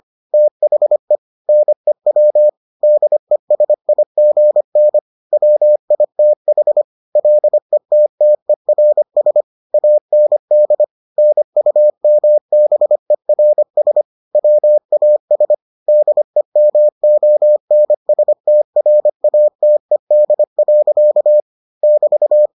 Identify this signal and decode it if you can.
Morse code, heard as a keyed tone, about 25 wpm, 600 Hz. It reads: THE NEW DESIGN WITH LETTERS AND NUMBERS WAS DEMONSTRATED. <BT>